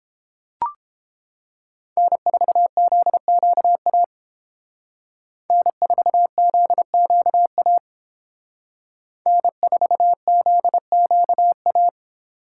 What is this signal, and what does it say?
Morse code, heard as a keyed tone, about 30 wpm, 700 Hz. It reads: N4ZQA N4ZQA N4ZQA